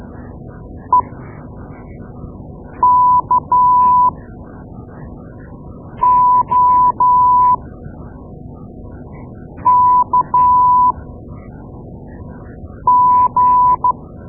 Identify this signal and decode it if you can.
Morse code, heard as a keyed tone, about 12 wpm, spaced as for 4 wpm, 990 Hz. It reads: EKOKG
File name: Morse Lessons / Morse Lessons / Tape-2A